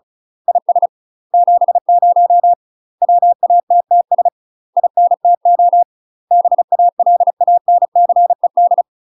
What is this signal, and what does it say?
Morse code, heard as a keyed tone, about 36 wpm, 700 Hz. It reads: IS 70 WATTS INTO BALANCED
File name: UK MASTER QSOs TRAINING / 35WPM / UK MASTER QSOs 0002